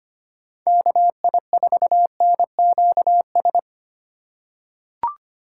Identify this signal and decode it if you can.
Morse code, heard as a keyed tone, about 25 wpm, 700 Hz. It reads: KI4NQS
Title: USA Call Signs - 6 Characters - 4x Character Spacing - 25wpm